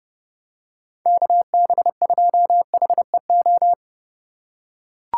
Morse code, heard as a keyed tone, about 28 wpm, 700 Hz. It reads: KB2HEO